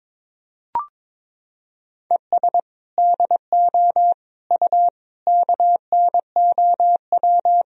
Morse code, heard as a keyed tone, about 22 wpm, 700 Hz. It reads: ES DO U KNOW
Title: Sentences from Top 200 Words and Abbreviations 22wpm